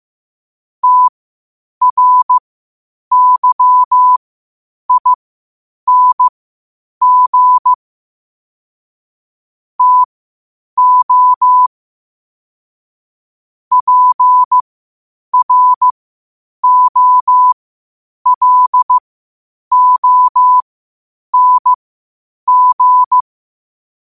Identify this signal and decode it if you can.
Morse code, heard as a keyed tone, about 15 wpm, spaced as for 8 wpm, 1000 Hz. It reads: TRYING TO PROLONG
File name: qotdfiles / 8wpm